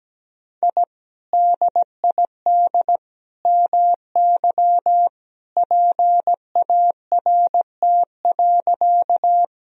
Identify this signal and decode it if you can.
Morse code, heard as a keyed tone, about 17 wpm, 700 Hz. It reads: I DID MY PART.
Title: Sentences from Top 1000 Words - Review Part 1 - 17wpm